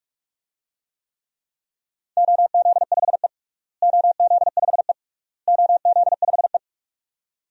Morse code, heard as a keyed tone, about 42 wpm, 700 Hz. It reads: OZ5E OZ5E OZ5E
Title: DXCC-100 Call Signs - 4 Characters - Repeated 3x - 45wpm